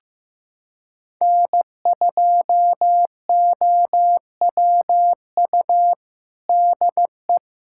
Morse code, heard as a keyed tone, about 15 wpm, 700 Hz. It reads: N2OWU DE
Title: USA Callsign DE Callsign - 5 Characters - 15wpm